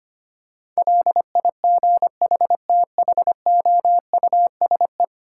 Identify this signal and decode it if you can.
Morse code, heard as a keyed tone, about 25 wpm, 700 Hz. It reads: LIGHTHOUSE
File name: practice / 25wpm / Difficult Words - Speed-Racing with Hints - 25wpm